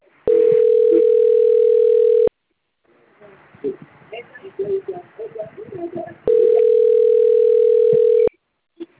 Ringback tone (North America), 440 + 480 Hz.
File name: out- - - - - .71809